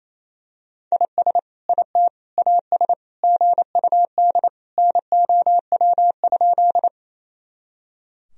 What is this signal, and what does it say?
Morse code, heard as a keyed tone, about 28 wpm, 700 Hz. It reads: IS IT AS GUD NOW?
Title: Sentences from Top 200 Words and Abbreviations 28wpm